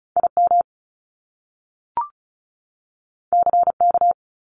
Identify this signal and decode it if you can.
Morse code, heard as a keyed tone, about 36 wpm, 700 Hz. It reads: IM CK
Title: Top Two Letter Combinations in Words 35wpm